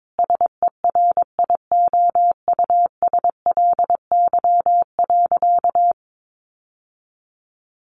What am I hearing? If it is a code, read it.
Morse code, heard as a keyed tone, about 22 wpm, 700 Hz. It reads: SERIOUSLY.
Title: Sentences from Top 1200 Words 22wpm